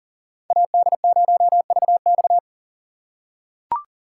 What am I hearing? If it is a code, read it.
Morse code, heard as a keyed tone, about 40 wpm, 700 Hz. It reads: AD0VX
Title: USA Call Signs - 5 Characters - 8x Character Spacing - 40wpm